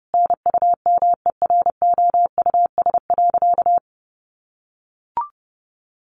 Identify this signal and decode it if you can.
Morse code, heard as a keyed tone, about 30 wpm, 700 Hz. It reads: NUMEROUS.
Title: Sentences from Top 1400 Words 30wpm